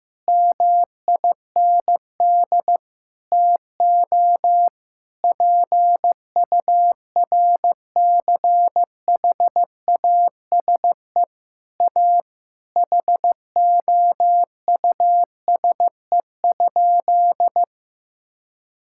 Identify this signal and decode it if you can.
Morse code, heard as a keyed tone, about 15 wpm, 700 Hz. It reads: MIND TO PURCHASE A HOUSE?
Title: Sentences from Top 1000 Words - Review Part 6 - 15wpm